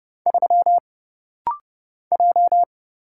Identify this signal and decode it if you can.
Morse code, heard as a keyed tone, about 29 wpm, 700 Hz. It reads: 3 J
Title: Single Letter-Number Alt - Rapid-Fire 30wpm